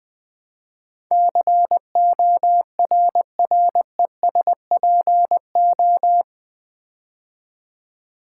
Morse code, heard as a keyed tone, about 20 wpm, 700 Hz. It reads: CORRESPO
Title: Build A Word - 4 Letters At A Time - 10-Letter Words - 20wpm